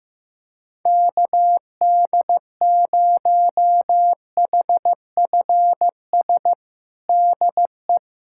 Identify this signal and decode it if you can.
Morse code, heard as a keyed tone, about 16 wpm, 700 Hz. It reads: KD0HFS DE
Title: USA Callsign DE Callsign - 6 Characters - 15wpm